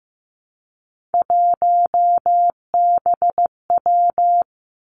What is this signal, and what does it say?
Morse code, heard as a keyed tone, about 15 wpm, 700 Hz. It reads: 1BW
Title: Three Challenging Letter-Number Combinations 15wpm